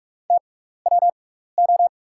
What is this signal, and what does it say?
Morse code, heard as a keyed tone, about 43 wpm, spaced as for 15 wpm, 690 Hz. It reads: TWO
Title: 6x Character Spacing - Top 100 Words - 45wpm